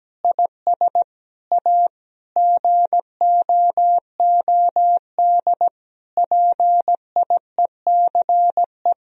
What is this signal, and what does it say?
Morse code, heard as a keyed tone, about 17 wpm, 700 Hz. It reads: IS A GOOD PIECE
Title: Sentences from Top 1000 Words - Review Part 3 - 17wpm